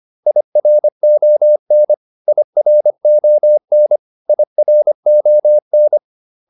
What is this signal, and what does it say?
Morse code, heard as a keyed tone, about 25 wpm, 600 Hz. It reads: IRON IRON IRON